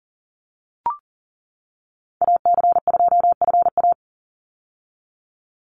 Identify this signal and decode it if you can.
Morse code, heard as a keyed tone, about 40 wpm, 700 Hz. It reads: AC2FA
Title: Call Signs - 40wpm